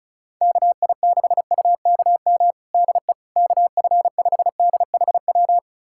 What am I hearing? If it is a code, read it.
Morse code, heard as a keyed tone, about 36 wpm, 700 Hz. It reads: KI6UKM DE KF5DHW
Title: USA Callsign DE Callsign - 6 Characters - 35wpm